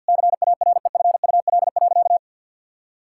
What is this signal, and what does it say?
Morse code, heard as a keyed tone, about 50 wpm, 700 Hz. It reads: CAREFUL.
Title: Sentences from Top 1600 Words 50wpm